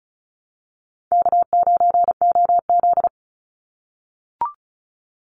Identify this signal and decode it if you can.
Morse code, heard as a keyed tone, about 34 wpm, 700 Hz. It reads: K9OZ
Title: NA SOTA Chaser Callsigns 35wpm